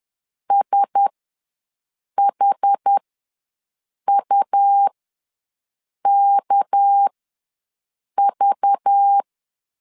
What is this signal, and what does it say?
Morse code, heard as a keyed tone, about 10 wpm, spaced as for 6 wpm, 800 Hz. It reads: SHUKV
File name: LEZIONI / 6 PKX0 / 6 PKX0 ws 10wpm 4220